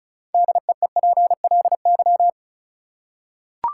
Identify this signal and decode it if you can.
Morse code, heard as a keyed tone, about 36 wpm, 700 Hz. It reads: DEEPLY
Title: Six Letter Words - 35wpm